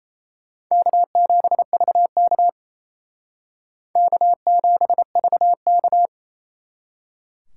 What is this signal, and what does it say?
Morse code, heard as a keyed tone, about 35 wpm, 700 Hz. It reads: K7VK K7VK